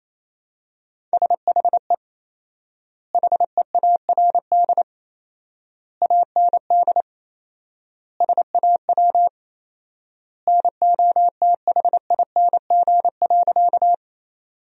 Morse code, heard as a keyed tone, about 28 wpm, 700 Hz. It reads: SHE HEARD AND SAW NOTHING.
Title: 4x Word-Spacing - Sentences from Top 500 Words - 28wpm